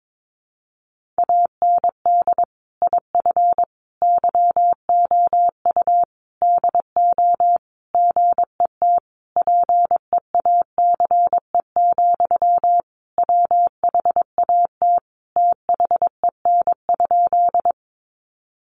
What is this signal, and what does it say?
Morse code, heard as a keyed tone, about 22 wpm, 700 Hz. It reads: AND IF YOU DO GET PEACE, WHAT THEN?
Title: Sentences from Top 1500 Words 22wpm